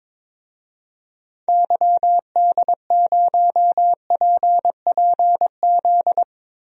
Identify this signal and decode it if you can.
Morse code, heard as a keyed tone, about 22 wpm, 700 Hz. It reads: YD0PPZ
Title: DXCC-100 Call Signs - 6 Characters - Repeated 3x - 22wpm